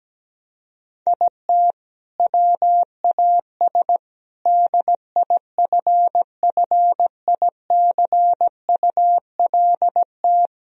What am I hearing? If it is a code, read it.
Morse code, heard as a keyed tone, about 17 wpm, 700 Hz. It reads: IT WAS DIFFICULT
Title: Sentences from Top 1000 Words - Review Part 1 - 17wpm